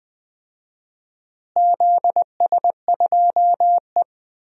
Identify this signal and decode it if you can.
Morse code, heard as a keyed tone, about 20 wpm, 700 Hz. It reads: ZS2E
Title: DXCC-100 Call Signs - 4 Characters - Repeated 3x - 20wpm